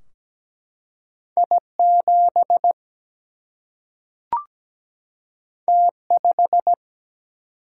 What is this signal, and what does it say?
Morse code, heard as a keyed tone, about 17 wpm, 700 Hz. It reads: I7 T5
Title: Single-Letter Single-Number 17wpm